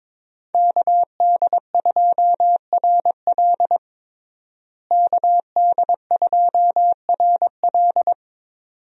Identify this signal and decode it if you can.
Morse code, heard as a keyed tone, about 22 wpm, 700 Hz. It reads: KD2RL KD2RL
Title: USA Call Signs - 5 Characters - Repeated 2x - 22wpm